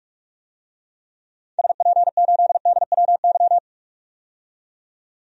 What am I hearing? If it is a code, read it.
Morse code, heard as a keyed tone, about 46 wpm, 700 Hz. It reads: SP8DWY
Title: DXCC-100 Call Signs - 6 Characters - 45wpm